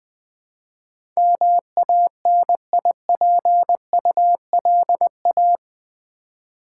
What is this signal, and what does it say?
Morse code, heard as a keyed tone, about 20 wpm, 700 Hz. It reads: MANIPULA
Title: Build A Word - 4 Letters At A Time - 10-Letter Words - 20wpm